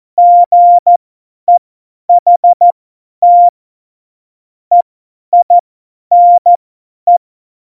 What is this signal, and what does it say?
Morse code, heard as a keyed tone, about 14 wpm, spaced as for 10 wpm, 700 Hz. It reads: GEHT EINE
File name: Morsekurs / Witze / Witz 28. August 2018, 50 F70